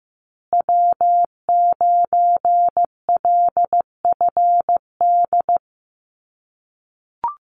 Morse code, heard as a keyed tone, about 15 wpm, 700 Hz. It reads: W9LFD
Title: USA Call Signs Repeated 2x 15wpm